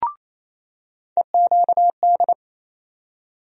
Morse code, heard as a keyed tone, about 28 wpm, 700 Hz. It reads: EQD
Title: USA Call Sign Postfixes - 3 Letters 28wpm